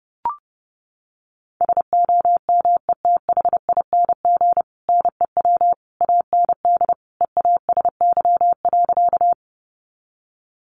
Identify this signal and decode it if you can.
Morse code, heard as a keyed tone, about 30 wpm, 700 Hz. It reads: SOMETHING NEW AND EASY.
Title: Sentences from Top 700 Words 30wpm